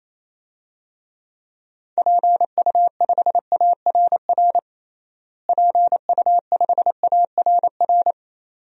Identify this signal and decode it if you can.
Morse code, heard as a keyed tone, about 28 wpm, 700 Hz. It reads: PU5ARR PU5ARR
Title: DXCC-100 Call Signs - 6 Characters - Repeated 2x - 28wpm